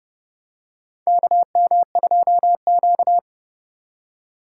Morse code, heard as a keyed tone, about 30 wpm, 700 Hz. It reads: KM2Q
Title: USA Call Signs - 4 Characters - 3x Character Spacing - 30wpm